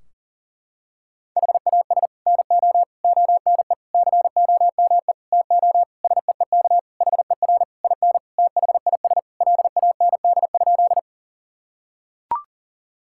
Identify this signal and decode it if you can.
Morse code, heard as a keyed tone, about 40 wpm, 700 Hz. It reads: HAS NO ONE COME TO SEEK HER IN THIS LAND?
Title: Sentences from Top 1600 Words 40wpm